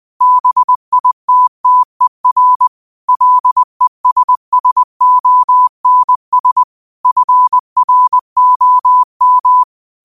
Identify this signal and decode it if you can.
Morse code, heard as a keyed tone, about 20 wpm, 1000 Hz. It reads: BITTER LESSONS FROM
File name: qotdfiles / 20wpm